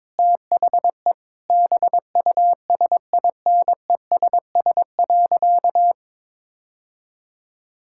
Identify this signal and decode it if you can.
Morse code, heard as a keyed tone, about 22 wpm, 700 Hz. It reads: THE BUSINESS.
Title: Sentences from Top 1000 Words - Review Part 6 - 22wpm